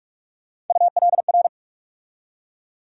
Morse code, heard as a keyed tone, about 47 wpm, 700 Hz. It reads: ULR
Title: USA Call Sign Postfixes - 3 Letters 45wpm